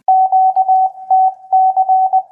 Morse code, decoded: QTC